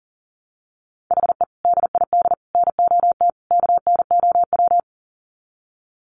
Morse code, heard as a keyed tone, about 40 wpm, 700 Hz. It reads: HE DID NOT KNOW
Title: Sets of 4 Words 40wpm